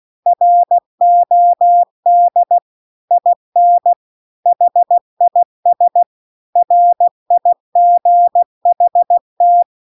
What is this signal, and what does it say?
Morse code, heard as a keyed tone, about 16 wpm, 700 Hz. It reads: ROD IN HIS RIGHT